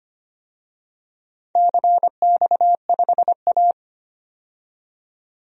Morse code, heard as a keyed tone, about 25 wpm, 700 Hz. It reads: CX5A
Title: DXCC-100 Call Signs - 4 Characters - 25wpm